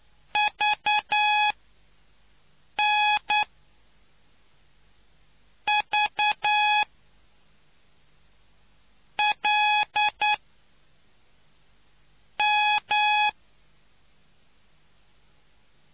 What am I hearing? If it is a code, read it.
Morse code, heard as a keyed tone, about 9 wpm, spaced as for 4 wpm, 840 Hz. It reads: VNVLM